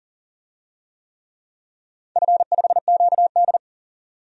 Morse code, heard as a keyed tone, about 41 wpm, 700 Hz. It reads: F5QD
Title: DXCC-100 Call Signs - 4 Characters - 40wpm